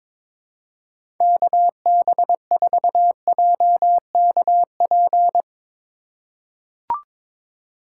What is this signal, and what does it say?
Morse code, heard as a keyed tone, about 22 wpm, 700 Hz. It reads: KB4JKP
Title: USA Call Signs - 6 Characters - 4x Character Spacing - 22wpm